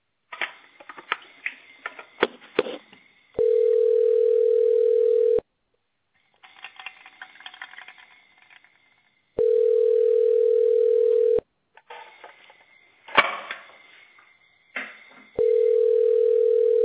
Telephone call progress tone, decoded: ringback tone